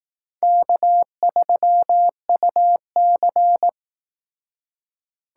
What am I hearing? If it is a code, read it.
Morse code, heard as a keyed tone, about 18 wpm, 700 Hz. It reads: K3UC
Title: Speed Racing - USA 1x2 Call Signs 15wpm